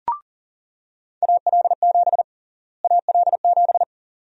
Morse code, heard as a keyed tone, about 37 wpm, 700 Hz. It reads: AL7 AL7
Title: USA Call Sign Prefixes - Repeated 2x - 40wpm